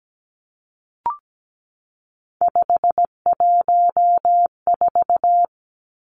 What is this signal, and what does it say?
Morse code, heard as a keyed tone, about 17 wpm, 700 Hz. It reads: 514
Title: Numbers - 3-Digit 17wpm